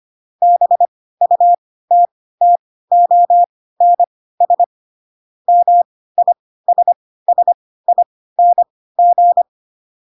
Morse code, heard as a keyed tone, about 25 wpm, spaced as for 16 wpm, 700 Hz. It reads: BUTTONS MISSING
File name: Pinocchio / Pinocchio ch2 10 25 ik1hge